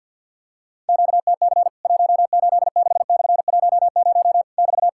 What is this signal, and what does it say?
Morse code, heard as a keyed tone, about 54 wpm, 700 Hz. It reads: QTC 186/10 <BT>
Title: Worked All Europe DX Contest - QTC Exchange with Repeats - 50wpm